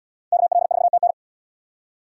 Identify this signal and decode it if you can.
Morse code, heard as a keyed tone, about 67 wpm, 700 Hz. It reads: KL4TN